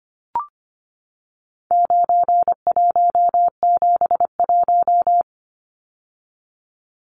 Morse code, heard as a keyed tone, about 25 wpm, 700 Hz. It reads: 9171
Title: Numbers - 4-Digit 25wpm